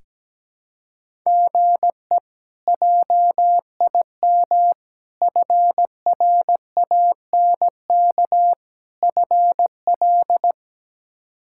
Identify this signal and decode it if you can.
Morse code, heard as a keyed tone, about 17 wpm, 700 Hz. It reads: GE JIM FRANK FL
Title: K1USN SST QSO Practice - 17wpm